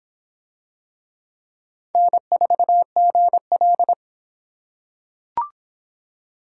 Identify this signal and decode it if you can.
Morse code, heard as a keyed tone, about 24 wpm, 700 Hz. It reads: N4GL